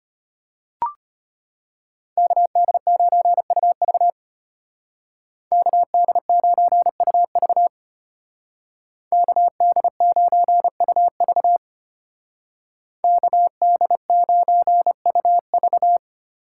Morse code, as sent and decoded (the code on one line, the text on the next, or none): -.- -.. ----. ..- ...- / -.- -.. ----. ..- ...- / -.- -.. ----. ..- ...- / -.- -.. ----. ..- ...-
KD9UV KD9UV KD9UV KD9UV